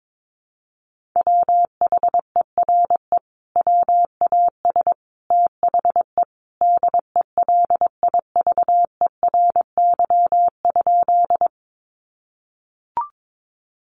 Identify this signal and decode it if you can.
Morse code, heard as a keyed tone, about 22 wpm, 700 Hz. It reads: WHERE WAS THE DELIVERY?
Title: Sentences from Top 1500 Words 22wpm